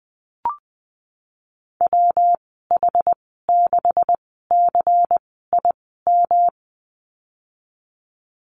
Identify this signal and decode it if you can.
Morse code, heard as a keyed tone, about 20 wpm, spaced as for 15 wpm, 700 Hz. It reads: WH6CIM
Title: 179 Lesson 29g - Slash - Call Signs 20wpm